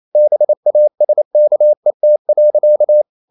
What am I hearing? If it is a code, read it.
Morse code, heard as a keyed tone, about 28 wpm, 600 Hz. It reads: BASKET.
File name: CW / Practice / EnglishSayings28